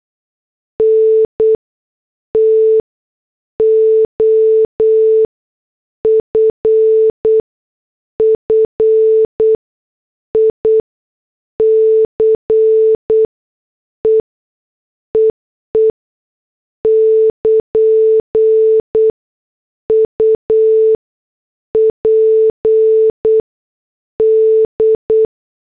Morse code, heard as a keyed tone, about 8 wpm, spaced as for 6 wpm, 440 Hz. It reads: NT OFFICE EE <KN> UPD
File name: morse / headlines